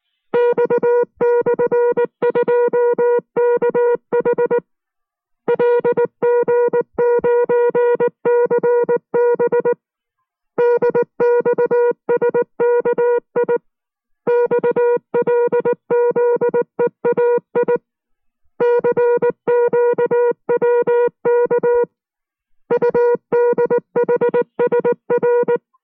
Morse code, heard as a keyed tone, about 19 wpm, 480 Hz. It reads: X/2KH LG9CB DXSKI XLZEAI CQWK UDHSR